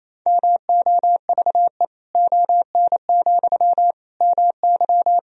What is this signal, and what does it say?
Morse code, heard as a keyed tone, about 28 wpm, 700 Hz. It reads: MOVE ON, MY
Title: Sentences from Top 500 Words and Abbreviations 28wpm